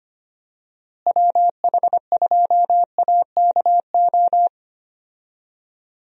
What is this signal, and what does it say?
Morse code, heard as a keyed tone, about 25 wpm, 700 Hz. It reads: WH2AKO
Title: USA Call Signs - 6 Characters - Repeated 2x - 25wpm